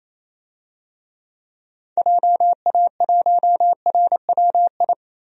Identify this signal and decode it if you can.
Morse code, heard as a keyed tone, about 28 wpm, 700 Hz. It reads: JA1RWI
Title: USA Call Signs - 6 Characters - Repeated 2x - 28wpm